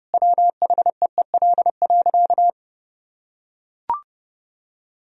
Morse code, heard as a keyed tone, about 32 wpm, 700 Hz. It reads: WHEEL.
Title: Sentences from Top 1000 Words - Review Part 3 - 30wpm